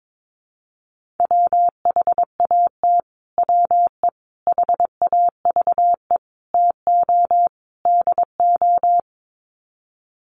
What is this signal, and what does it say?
Morse code, heard as a keyed tone, about 22 wpm, 700 Hz. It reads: WHAT WE HAVE TO DO
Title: Sets of 5 Words Encore 22wpm